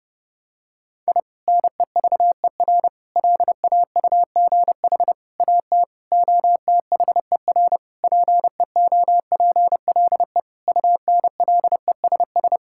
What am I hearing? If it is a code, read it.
Morse code, heard as a keyed tone, about 30 wpm, 700 Hz. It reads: I NEVER LAUGH AT OTHER PEOPLE UNLESS
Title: Sentences from Top 1000 Words - Review Part 6 - 30wpm